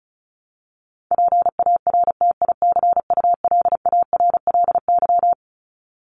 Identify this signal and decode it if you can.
Morse code, heard as a keyed tone, about 34 wpm, 700 Hz. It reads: PARTICULARLY
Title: LY Suffix - Top 100 Words with Base Words 35wpm